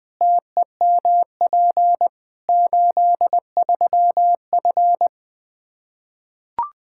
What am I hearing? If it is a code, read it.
Morse code, heard as a keyed tone, about 20 wpm, 700 Hz. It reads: TEMP 83F
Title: Practice QSO - Exchange 2 - Info - 20wpm